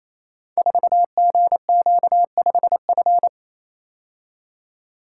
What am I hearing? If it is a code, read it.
Morse code, heard as a keyed tone, about 28 wpm, 700 Hz. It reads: 4GQ5F
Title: Code Groups - 5 Letter:Numbers - 28wpm